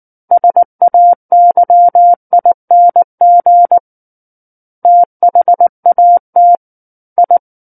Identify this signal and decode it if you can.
Morse code, heard as a keyed tone, about 19 wpm, 700 Hz. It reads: SAYING THAT I
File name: Book One / 10-Chapter 10 - 19 WPM